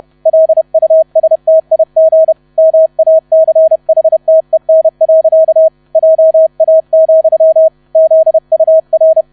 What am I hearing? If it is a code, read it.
Morse code, heard as a keyed tone, about 31 wpm, 640 Hz, with radio noise. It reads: LUSTIG MACHTEN. JA, ZUR
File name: 30wpm